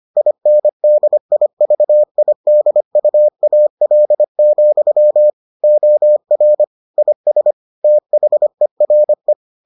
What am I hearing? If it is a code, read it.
Morse code, heard as a keyed tone, about 25 wpm, 600 Hz. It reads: INDIVIDUAL, OR IS THERE